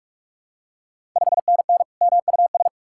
Morse code, heard as a keyed tone, about 42 wpm, 700 Hz. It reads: 5NN MUS
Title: Ontario QSO Party Exchange 45wpm